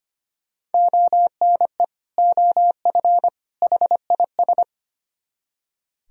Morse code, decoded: ONE OF HIS

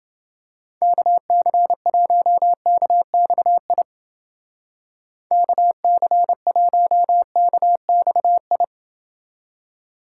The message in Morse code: KC1KXI KC1KXI

